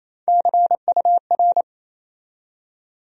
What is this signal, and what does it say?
Morse code, heard as a keyed tone, about 28 wpm, 700 Hz. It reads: CUR
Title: USA Call Sign Postfixes - 3 Letters 28wpm